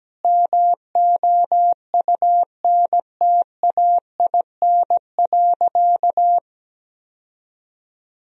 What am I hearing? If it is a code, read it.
Morse code, heard as a keyed tone, about 17 wpm, 700 Hz. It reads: MOUNTAIN.
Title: Sentences from Top 1000 Words - Review Part 1 - 17wpm